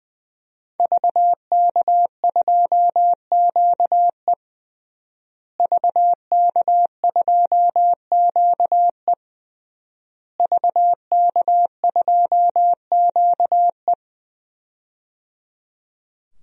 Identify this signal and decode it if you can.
Morse code, heard as a keyed tone, about 20 wpm, 700 Hz. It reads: VK2QE VK2QE VK2QE
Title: DXCC-100 Call Signs - 5 Characters - Repeated 3x - 20wpm